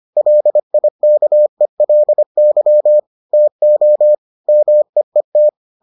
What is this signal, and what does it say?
Morse code, heard as a keyed tone, about 25 wpm, 600 Hz. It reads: LIKELY TO MEET